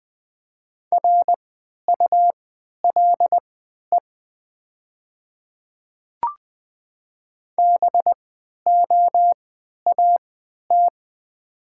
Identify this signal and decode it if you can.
Morse code, heard as a keyed tone, about 20 wpm, spaced as for 11 wpm, 700 Hz. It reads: RULE BOAT
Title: 3x Character Spacing - Top 500 Words - 20wpm